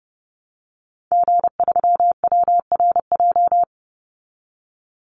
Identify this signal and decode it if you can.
Morse code, heard as a keyed tone, about 30 wpm, 700 Hz. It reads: G3WRJ
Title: Call Signs Repeated 3x - 30wpm